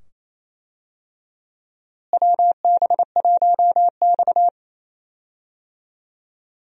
Morse code, heard as a keyed tone, about 28 wpm, 700 Hz. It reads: WB1X
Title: USA Call Signs - 4 Characters - Repeated 2x - 28wpm